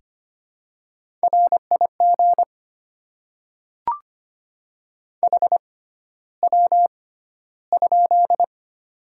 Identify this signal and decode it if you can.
Morse code, heard as a keyed tone, about 25 wpm, spaced as for 13 wpm, 700 Hz. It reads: RIG HW?
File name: Top 20 QSO Elements - 6x Character Spacing - 25wpm